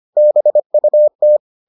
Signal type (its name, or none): Morse code